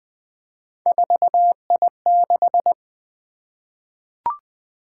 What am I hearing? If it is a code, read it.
Morse code, heard as a keyed tone, about 20 wpm, 700 Hz. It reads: 4I6
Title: Three Letter-Number Combinations - 20wpm